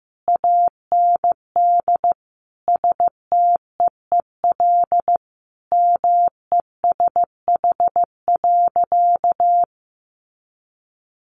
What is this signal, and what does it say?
Morse code, heard as a keyed tone, about 15 wpm, 700 Hz. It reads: AND STEEL MESH.
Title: A Sound Of Thunder 15wpm